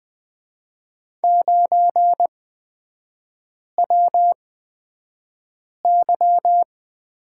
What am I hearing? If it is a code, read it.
Morse code, heard as a keyed tone, about 20 wpm, spaced as for 5 wpm, 700 Hz. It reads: 9WY
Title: 224 Lesson 36e - BT - Three Characters 20wpm